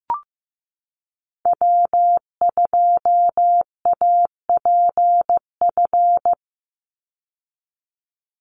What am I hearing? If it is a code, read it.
Morse code, heard as a keyed tone, about 15 wpm, 700 Hz. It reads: W2APF
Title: Call Signs Repeated 3x - 15wpm